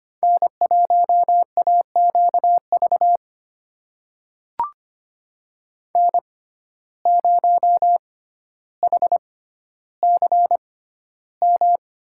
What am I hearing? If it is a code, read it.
Morse code, heard as a keyed tone, about 25 wpm, spaced as for 13 wpm, 700 Hz. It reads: N1AQV N0HCM
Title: USA Call Signs - 5 Characters - 6x Character Spacing - 25wpm